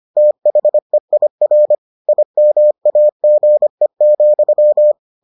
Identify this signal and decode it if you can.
Morse code, heard as a keyed tone, about 25 wpm, 600 Hz. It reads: THEIR IMAGE,